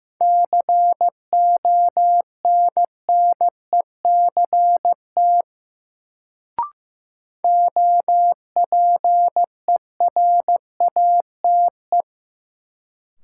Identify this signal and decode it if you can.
Morse code, heard as a keyed tone, about 15 wpm, 700 Hz. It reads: CONNECT OPERATE
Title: Sentences from Top 1500 Words - New Word Drill Alt - 15wpm